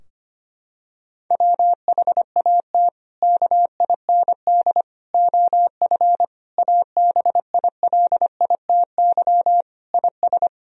Morse code, heard as a keyed tone, about 25 wpm, 700 Hz. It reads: WHAT KIND OF ABILITY IS
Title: Sentences from Top 1000 Words - Review Part 3 - 25wpm